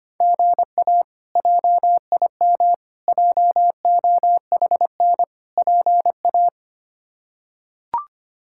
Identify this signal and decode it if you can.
Morse code, heard as a keyed tone, about 25 wpm, 700 Hz. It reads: GA JIM JOHN PA
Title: K1USN SST QSO Practice - 25wpm